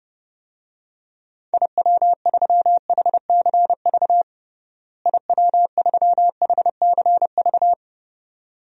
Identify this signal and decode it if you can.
Morse code, heard as a keyed tone, about 32 wpm, 700 Hz. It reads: IW3HCV IW3HCV
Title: DXCC-100 Call Signs - 6 Characters - Repeated 2x - 30wpm